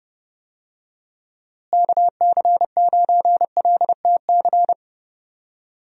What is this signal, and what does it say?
Morse code, heard as a keyed tone, about 32 wpm, 700 Hz. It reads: KC9LTC
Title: USA Call Signs - 6 Characters - Repeated 2x - 30wpm